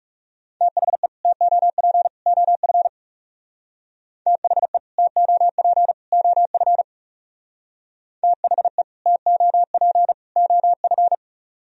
Morse code, heard as a keyed tone, about 36 wpm, 700 Hz. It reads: THE TOP OF THE TOP OF THE TOP OF
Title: Speed-Racing - Sets of 3 Words 30wpm